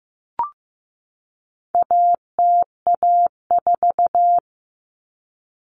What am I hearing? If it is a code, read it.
Morse code, heard as a keyed tone, about 15 wpm, 700 Hz. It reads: ATA4
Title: Common Cut Numbers - 4-Digit 15wpm